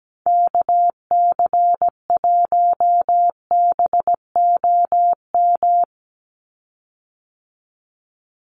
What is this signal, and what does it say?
Morse code, heard as a keyed tone, about 17 wpm, 700 Hz. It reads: KC1BOM
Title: USA Call Signs 17wpm